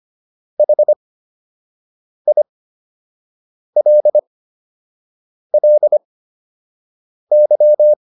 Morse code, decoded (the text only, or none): HILLY